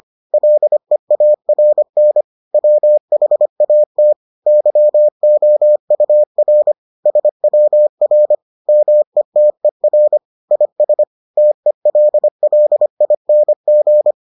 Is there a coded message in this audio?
Yes: Morse code from 0.3 to 14.1 s, heard as a keyed tone, about 25 wpm, 600 Hz. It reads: LEARN WHAT YOUR SWR METER IS TELLING